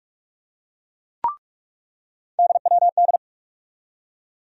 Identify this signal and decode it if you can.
Morse code, heard as a keyed tone, about 47 wpm, 700 Hz. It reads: DWD
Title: USA Call Sign Postfixes - 3 Letters 45wpm